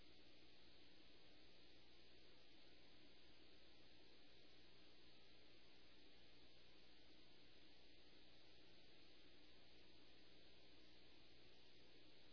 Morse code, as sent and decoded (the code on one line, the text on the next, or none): none